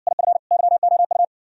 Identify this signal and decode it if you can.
Morse code, heard as a keyed tone, about 55 wpm, 700 Hz. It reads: IF YOU